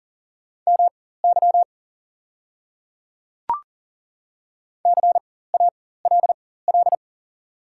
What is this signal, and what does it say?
Morse code, heard as a keyed tone, about 40 wpm, spaced as for 19 wpm, 690 Hz. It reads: MY CALL
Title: 4x Character Spacing - Top 100 Words - 40wpm